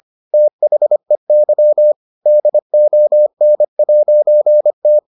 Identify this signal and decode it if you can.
Morse code, heard as a keyed tone, about 25 wpm, 600 Hz. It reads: THEY DON'T